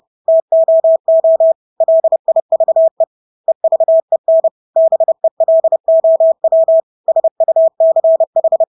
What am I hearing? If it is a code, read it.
Morse code, heard as a keyed tone, about 30 wpm, 650 Hz. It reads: TOO LIVE EVEN BELOW SUCH
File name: Engelske ord 30-30 wpm